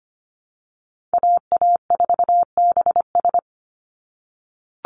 Morse code, heard as a keyed tone, about 25 wpm, 700 Hz. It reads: AA4BS
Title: USA Call Signs 25wpm